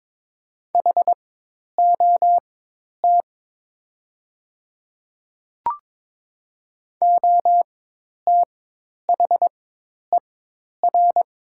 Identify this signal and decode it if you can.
Morse code, heard as a keyed tone, about 22 wpm, spaced as for 10 wpm, 700 Hz. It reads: HOT OTHER
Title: 4x Character Spacing - Top 100 Words - 22wpm